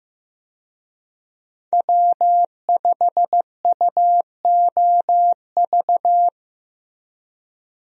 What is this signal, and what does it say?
Morse code, heard as a keyed tone, about 15 wpm, 700 Hz. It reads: W5UOV
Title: USA Call Signs - 5 Characters - Repeated 2x - 15wpm